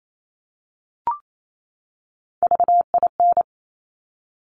Morse code, heard as a keyed tone, about 28 wpm, 700 Hz. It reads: VIN
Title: Top Three Letter Combinations in Words 28wpm